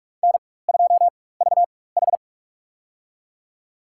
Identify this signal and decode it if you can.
Morse code, heard as a keyed tone, about 46 wpm, spaced as for 21 wpm, 700 Hz. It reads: N2VH
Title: USA Call Signs - 4 Characters - 4x Character Spacing - 45wpm